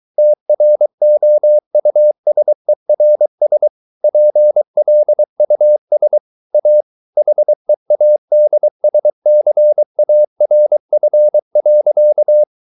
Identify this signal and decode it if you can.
Morse code, heard as a keyed tone, about 23 wpm, 600 Hz. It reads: TROUSERS PLUS A HEADSCARF.
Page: Romanen Never af Ken Follet på engelsk.